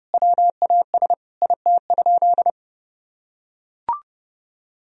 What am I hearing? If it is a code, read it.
Morse code, heard as a keyed tone, about 28 wpm, 700 Hz. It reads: WAS IT?
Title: Sentences from Top 1000 Words - Review Part 4 - 30wpm